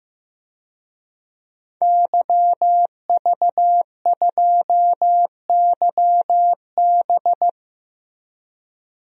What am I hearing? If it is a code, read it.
Morse code, heard as a keyed tone, about 15 wpm, 700 Hz. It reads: YV2YB
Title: DXCC-100 Call Signs - 5 Characters - Repeated 3x - 15wpm